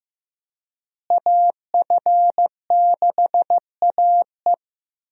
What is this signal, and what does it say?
Morse code, heard as a keyed tone, about 16 wpm, 700 Hz. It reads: AF6AE
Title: USA Call Signs - 5 Characters - Repeated 3x - 15wpm